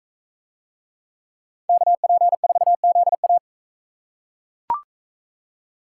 Morse code, heard as a keyed tone, about 40 wpm, 700 Hz. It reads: KP4ZA